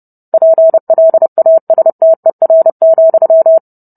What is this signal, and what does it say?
Morse code, heard as a keyed tone, about 30 wpm, 650 Hz. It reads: PLASTER,